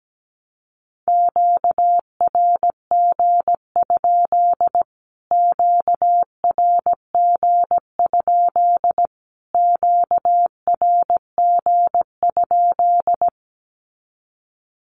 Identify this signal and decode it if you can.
Morse code, heard as a keyed tone, about 17 wpm, 700 Hz. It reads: QRG? QRG? QRG?
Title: Q-Codes Repeated 3x 17wpm